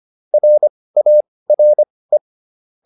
Morse code, heard as a keyed tone, about 25 wpm, spaced as for 18 wpm, 600 Hz. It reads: RARE